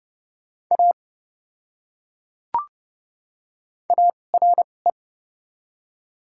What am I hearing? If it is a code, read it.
Morse code, heard as a keyed tone, about 29 wpm, spaced as for 22 wpm, 700 Hz. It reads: A ARE
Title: 2x Character Spacing - Top 200 Words - 30wpm